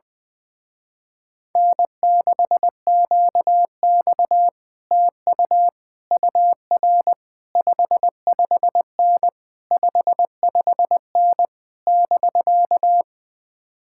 Morse code, heard as a keyed tone, about 20 wpm, 700 Hz. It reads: N6QX TU UR 55N 55N <BK>
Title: Speed-Racing Common QSO Phrases - 20wpm